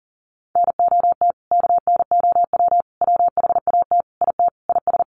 Morse code, heard as a keyed tone, about 40 wpm, 700 Hz. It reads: NOT KNOW WHAT IT IS
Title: Sets of 5 Words Encore 40wpm